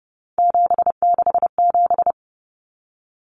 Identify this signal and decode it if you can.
Morse code, heard as a keyed tone, about 30 wpm, 700 Hz. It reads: 767
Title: Numbers - 3-Digit 30wpm